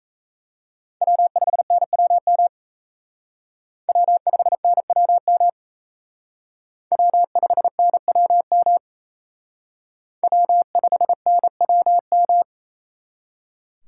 Morse code, heard as a keyed tone, about 35 wpm, 700 Hz. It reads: W5NWM W5NWM W5NWM W5NWM
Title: Speed Racing - USA 1x3 Call Signs 28wpm